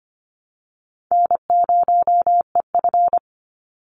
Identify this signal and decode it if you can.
Morse code, heard as a keyed tone, about 25 wpm, 700 Hz. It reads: N0EF
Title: Call Signs Repeated 2x - 25wpm